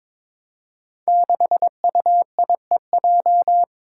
Morse code, heard as a keyed tone, about 22 wpm, 700 Hz. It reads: 6UIEJ
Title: Five Challenging Letter-Number Combinations 22wpm